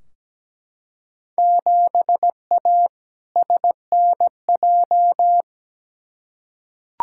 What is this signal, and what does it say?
Morse code, heard as a keyed tone, about 17 wpm, 700 Hz. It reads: 7A SNJ
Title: Field Day Exchanges Encore - 17wpm